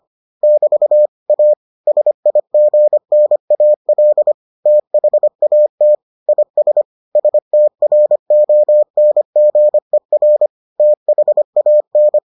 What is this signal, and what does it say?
Morse code, heard as a keyed tone, about 25 wpm, 600 Hz. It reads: <BT> A SIGNAL THAT IS STRONGER THAN